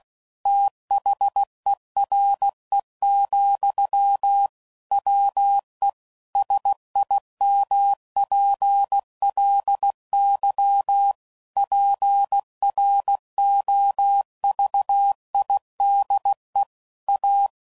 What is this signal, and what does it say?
Morse code, heard as a keyed tone, about 16 wpm, 800 Hz. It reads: THERE, WE SIMPLY PROVIDE A
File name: CW Code Groups @15WPM / Technical Subjects / PwrSupply